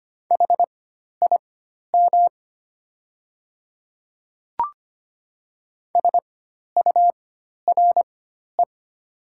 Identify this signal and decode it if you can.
Morse code, heard as a keyed tone, about 25 wpm, spaced as for 12 wpm, 700 Hz. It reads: HIM SURE
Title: 4x Character Spacing - Top 300 Words - 25wpm